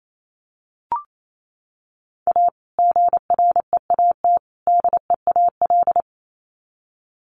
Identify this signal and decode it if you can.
Morse code, heard as a keyed tone, about 28 wpm, 700 Hz. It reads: A GREAT DEAL
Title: Sets of 3 Words 28wpm